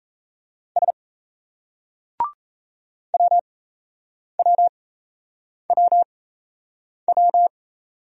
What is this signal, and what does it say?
Morse code, heard as a keyed tone, about 37 wpm, spaced as for 8 wpm, 700 Hz. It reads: S WWWW